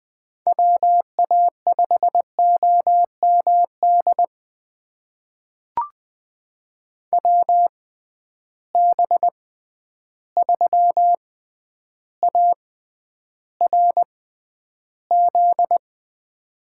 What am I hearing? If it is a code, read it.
Morse code, heard as a keyed tone, about 20 wpm, spaced as for 10 wpm, 700 Hz. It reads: WA5OMD WB3ARZ
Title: USA Call Signs - 6 Characters - 6x Character Spacing - 20wpm